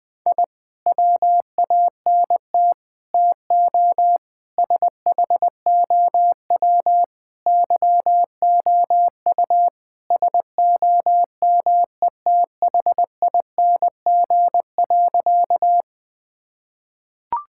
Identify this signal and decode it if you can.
Morse code, heard as a keyed tone, about 20 wpm, 700 Hz. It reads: I WANT TO SHOW YOU SOMETHING.
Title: Sentences with 5-Word Phrases - 20wpm